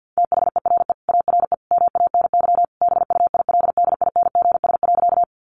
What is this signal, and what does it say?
Morse code, heard as a keyed tone, about 48 wpm, 700 Hz. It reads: THERE ARE MANY BUILDINGS.
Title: Easy Phrases 50wpm